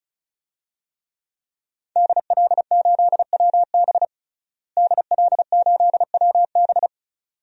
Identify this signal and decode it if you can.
Morse code, heard as a keyed tone, about 34 wpm, 700 Hz. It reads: DL8WB DL8WB
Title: DXCC-100 Call Signs - 5 Characters - Repeated 2x - 35wpm